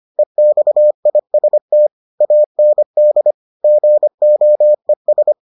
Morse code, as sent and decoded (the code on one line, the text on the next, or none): . -..- .. ... - / .- -. -.. / --. --- . ...
EXIST AND GOES